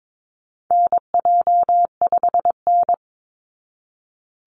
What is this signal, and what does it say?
Morse code, heard as a keyed tone, about 22 wpm, 700 Hz. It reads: NJ5N
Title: Call Signs Repeated 3x - 22wpm